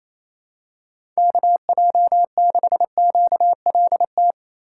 Morse code, heard as a keyed tone, about 28 wpm, 700 Hz. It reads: KJ6QLT
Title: USA Call Signs - 6 Characters - 3x Character Spacing - 28wpm